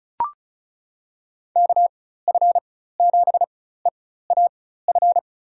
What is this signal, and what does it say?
Morse code, heard as a keyed tone, about 36 wpm, spaced as for 16 wpm, 700 Hz. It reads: KF7EAF
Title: USA Call Signs - 6 Characters - 4x Character Spacing - 35wpm